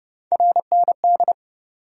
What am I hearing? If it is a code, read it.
Morse code, heard as a keyed tone, about 30 wpm, 700 Hz. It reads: RND
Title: USA Call Sign Postfixes - 3 Letters 30wpm